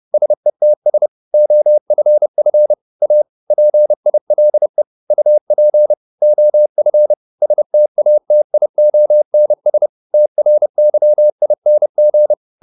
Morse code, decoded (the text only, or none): SETS OFF A PILE UP OF STATIONS TRYING